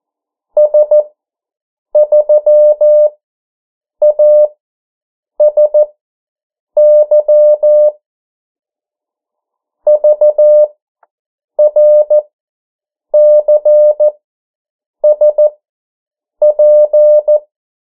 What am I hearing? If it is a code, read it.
Morse code, heard as a keyed tone, about 14 wpm, spaced as for 7 wpm, 600 Hz. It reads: S3ASY VRCSP
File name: morse mp3 / 09 urok 8